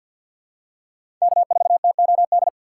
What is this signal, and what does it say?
Morse code, heard as a keyed tone, about 47 wpm, 700 Hz. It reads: K4TOD
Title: USA Call Signs - 5 Characters - 4x Character Spacing - 50wpm